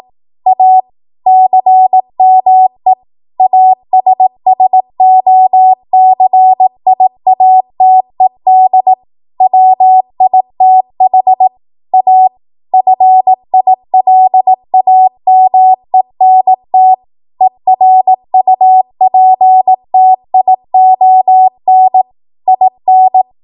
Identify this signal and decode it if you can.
Morse code, heard as a keyed tone, about 18 wpm, 750 Hz. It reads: A CME ASSOCIATED WITH A FILAMENT ERUPTION IN
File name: Morse / 18WPM